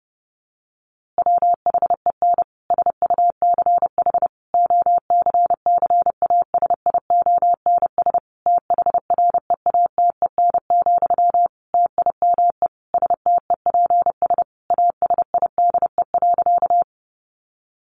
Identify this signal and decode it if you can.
Morse code, heard as a keyed tone, about 30 wpm, 700 Hz. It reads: WHEN SUCH OCCASIONS THREATEN, TIME STEPS ASIDE.
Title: A Sound Of Thunder 30wpm